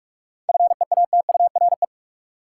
Morse code, heard as a keyed tone, about 46 wpm, 700 Hz. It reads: FEATURE
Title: Sentences from Top 800 Words - New Word Drill - 45wpm